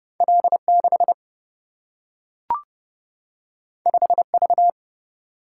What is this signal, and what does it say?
Morse code, heard as a keyed tone, about 30 wpm, 700 Hz. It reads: L 6 5 V
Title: Single-Letter Single-Number 30wpm